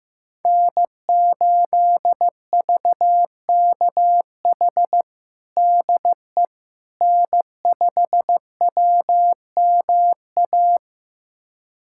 Morse code, heard as a keyed tone, about 16 wpm, 700 Hz. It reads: N8VKH DE N5WMA
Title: USA Callsign DE Callsign - 5 Characters - 15wpm